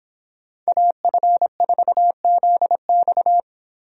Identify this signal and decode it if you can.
Morse code, heard as a keyed tone, about 31 wpm, 700 Hz. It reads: AF4ZX